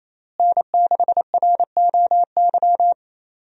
Morse code, heard as a keyed tone, about 28 wpm, 700 Hz. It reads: N6ROY
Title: USA Call Signs - 5 Characters - Repeated 3x - 28wpm